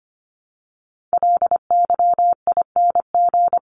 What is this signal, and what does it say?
Morse code, heard as a keyed tone, about 25 wpm, 700 Hz. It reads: LYING
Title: ING Suffix - Top 100 Words with Base Words 25wpm